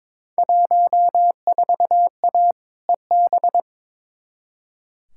Morse code, heard as a keyed tone, about 22 wpm, 700 Hz. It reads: 14A EB
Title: Field Day Exchanges Encore - 22wpm